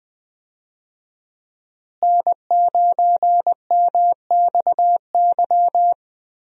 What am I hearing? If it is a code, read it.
Morse code, heard as a keyed tone, about 20 wpm, 700 Hz. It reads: N9MXY
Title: USA Call Signs - 5 Characters - Repeated 3x - 20wpm